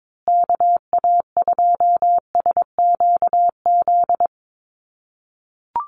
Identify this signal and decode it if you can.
Morse code, heard as a keyed tone, about 22 wpm, 700 Hz. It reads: KA2SQZ
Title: USA Call Signs Repeated 2x 22wpm